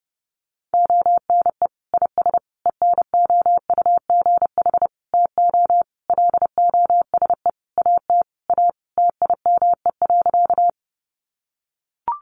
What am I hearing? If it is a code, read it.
Morse code, heard as a keyed tone, about 30 wpm, 700 Hz. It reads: ONE IS ENOUGH TO LOSE AT A TIME.
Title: Sentences from Top 1100 Words 30wpm